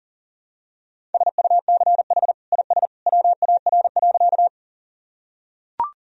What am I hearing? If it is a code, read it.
Morse code, heard as a keyed tone, about 40 wpm, 700 Hz. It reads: SUCH IS WAR.
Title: Sentences from Top 1000 Words - Review Part 2 - 40wpm